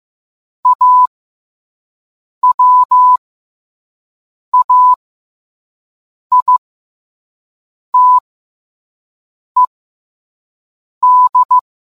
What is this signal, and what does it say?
Morse code, heard as a keyed tone, about 15 wpm, spaced as for 5 wpm, 1000 Hz. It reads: AWAITED